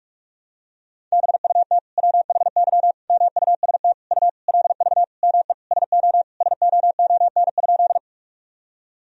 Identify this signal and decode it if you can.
Morse code, heard as a keyed tone, about 47 wpm, 700 Hz. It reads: BUT WHY MUST U LV ME SO SOON?
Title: Sentences from Top 400 Words and Abbreviations 45wpm